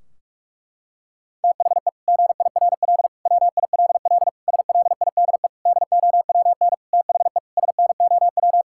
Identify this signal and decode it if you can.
Morse code, heard as a keyed tone, about 46 wpm, 700 Hz. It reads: THE GIRL WILL SLIDE DOWN THE SNOW
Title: Sentences from Top 1600 Words 45wpm